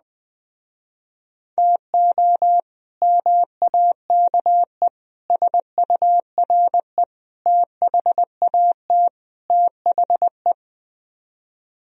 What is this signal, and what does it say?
Morse code, heard as a keyed tone, about 20 wpm, 700 Hz. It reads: TO MAKE SURE THAT THE
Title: Speed-Racing - Sets of 5 Words 17wpm